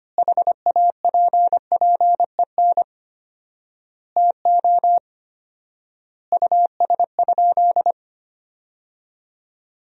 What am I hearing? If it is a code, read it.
Morse code, heard as a keyed tone, about 25 wpm, 700 Hz. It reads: HAPPEN TO US?
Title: 4x Word-Spacing - Sentences from Top 400 Words - 25wpm